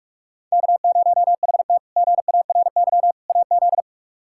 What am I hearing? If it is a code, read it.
Morse code, heard as a keyed tone, about 46 wpm, 700 Hz. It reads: K0HT GARY AZ
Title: K1USN SST QSO Practice - 45wpm